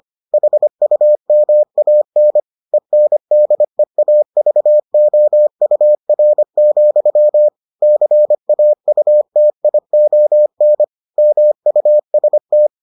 Morse code, heard as a keyed tone, about 25 wpm, 600 Hz. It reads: HUMAN ENDEAVOUR, CAUTION MUST